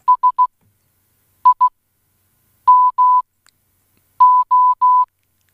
Morse code, heard as a keyed tone, about 16 wpm, 1000 Hz. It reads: S I M O